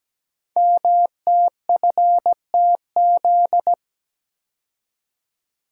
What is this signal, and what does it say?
Morse code, heard as a keyed tone, about 17 wpm, 700 Hz. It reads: MTFTZ
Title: Code Groups - 5 Letter:Numbers - 17wpm